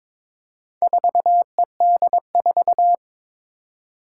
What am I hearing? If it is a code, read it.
Morse code, heard as a keyed tone, about 22 wpm, 700 Hz. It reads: 4ED4
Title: Four Challenging Letter-Number Combinations 22wpm